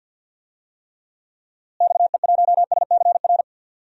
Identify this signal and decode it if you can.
Morse code, heard as a keyed tone, about 48 wpm, 700 Hz. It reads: XE1SCR
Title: DXCC-100 Call Signs - 6 Characters - Repeated 3x - 50wpm